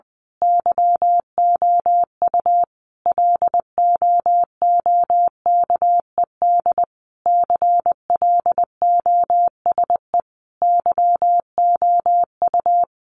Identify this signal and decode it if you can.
Morse code, heard as a keyed tone, about 20 wpm, 700 Hz. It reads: YOU LOOKED CLOSE YOU
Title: A Sound Of Thunder 20wpm